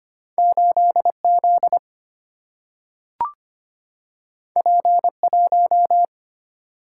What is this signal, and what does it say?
Morse code, heard as a keyed tone, about 25 wpm, 700 Hz. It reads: 8Z P1
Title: Single-Letter Single-Number 25wpm